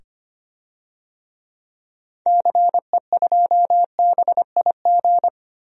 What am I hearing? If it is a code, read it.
Morse code, heard as a keyed tone, about 25 wpm, 700 Hz. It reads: CE2BIG
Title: DXCC-100 Call Signs - 6 Characters - 25wpm